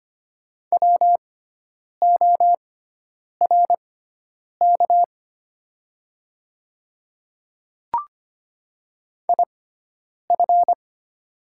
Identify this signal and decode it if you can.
Morse code, heard as a keyed tone, about 25 wpm, spaced as for 9 wpm, 700 Hz. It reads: WORK IF
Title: 6x Character Spacing - Top 500 Words - 25wpm